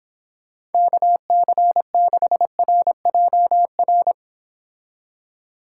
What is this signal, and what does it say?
Morse code, heard as a keyed tone, about 29 wpm, 700 Hz. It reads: KC6RJR